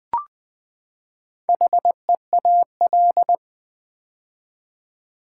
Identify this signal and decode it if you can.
Morse code, heard as a keyed tone, about 20 wpm, 700 Hz. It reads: HEAL
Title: Build A Word - 4 Letters At A Time - 10-Letter Words - 20wpm